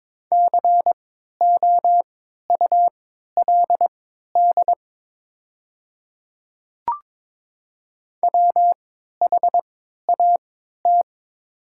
Morse code, heard as a keyed tone, about 22 wpm, spaced as for 13 wpm, 700 Hz. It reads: COULD WHAT
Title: 3x Character Spacing - Top 100 Words - 22wpm